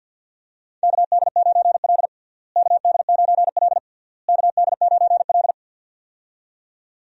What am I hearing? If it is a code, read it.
Morse code, heard as a keyed tone, about 52 wpm, 700 Hz. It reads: KD9L KD9L KD9L
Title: USA Call Signs - 4 Characters - Repeated 3x - 50wpm